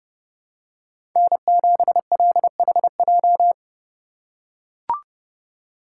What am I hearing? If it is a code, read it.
Morse code, heard as a keyed tone, about 30 wpm, 700 Hz. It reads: N7LHJ
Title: USA Call Signs - 5 Characters - 4x Character Spacing - 30wpm